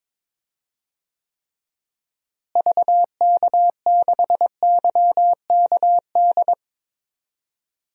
Morse code, heard as a keyed tone, about 21 wpm, 700 Hz. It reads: VK6YKD
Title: DXCC-100 Call Signs - 6 Characters - Repeated 2x - 22wpm